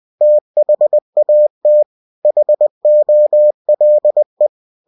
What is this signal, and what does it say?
Morse code, heard as a keyed tone, about 20 wpm, 600 Hz. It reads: THAT HOLE
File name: cw-books / winnie-the-pooh / winnie-20wpm-page-0013